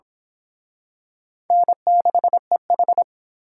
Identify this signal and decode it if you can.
Morse code, heard as a keyed tone, about 26 wpm, 700 Hz. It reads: N6EH
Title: Speed Racing - USA 1x2 Call Signs 22wpm